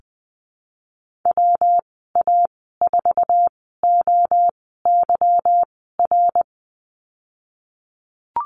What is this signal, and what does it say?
Morse code, heard as a keyed tone, about 20 wpm, spaced as for 15 wpm, 700 Hz. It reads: WA4OYR
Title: 147 Lesson 24g - Letter y - Call Signs 20wpm